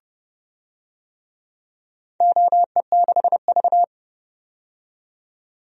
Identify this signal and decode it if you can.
Morse code, heard as a keyed tone, about 30 wpm, 700 Hz. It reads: OE6V
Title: DXCC-100 Call Signs - 4 Characters - 30wpm